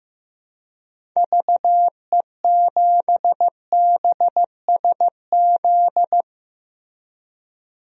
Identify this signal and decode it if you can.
Morse code, heard as a keyed tone, about 15 wpm, 700 Hz. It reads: VE7BSZ
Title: DXCC-100 Call Signs - 6 Characters - 15wpm